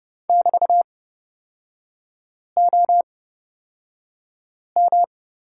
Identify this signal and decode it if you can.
Morse code, heard as a keyed tone, about 30 wpm, spaced as for 5 wpm, 700 Hz. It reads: <BT>OM
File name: Lesson 36e - BT - Three Characters 30wpm